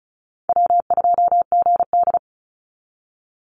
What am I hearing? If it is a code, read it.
Morse code, heard as a keyed tone, about 34 wpm, 700 Hz. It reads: W2GD
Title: Call Signs - 35wpm